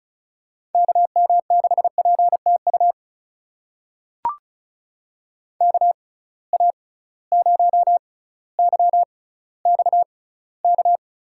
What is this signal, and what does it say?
Morse code, heard as a keyed tone, about 36 wpm, spaced as for 18 wpm, 700 Hz. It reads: KM6PTU KA0YXK
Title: USA Call Signs - 6 Characters - 6x Character Spacing - 35wpm